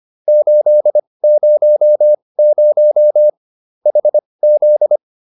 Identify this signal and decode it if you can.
Morse code, heard as a keyed tone, about 25 wpm, spaced as for 20 wpm, 600 Hz. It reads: 800 HZ